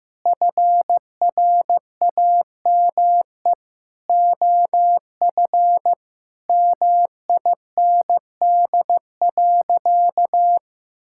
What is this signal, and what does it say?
Morse code, heard as a keyed tone, about 16 wpm, 700 Hz. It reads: FRAME OF MIND.
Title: Sentences from Top 1000 Words - Review Part 6 - 15wpm